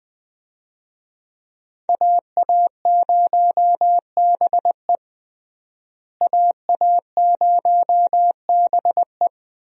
Morse code, heard as a keyed tone, about 20 wpm, 700 Hz. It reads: AA0BE AA0BE
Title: USA Call Signs - 5 Characters - Repeated 3x - 20wpm